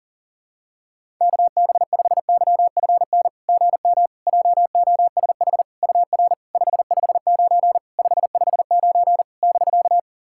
Morse code, heard as a keyed tone, about 41 wpm, 700 Hz. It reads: KB5YFN GM JOSH UR 559 559 <BK>
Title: SOTA QSO Practice 40wpm